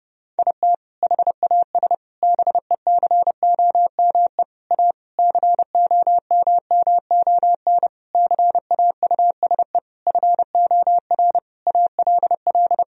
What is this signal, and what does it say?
Morse code, heard as a keyed tone, about 31 wpm, 700 Hz. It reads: IT HAS BECOME A COMMON CAUSE FOR ALL
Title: Sentences from Top 1000 Words - Review Part 3 - 30wpm